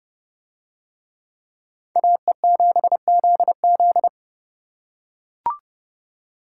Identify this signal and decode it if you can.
Morse code, heard as a keyed tone, about 27 wpm, 700 Hz. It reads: AE7ZZ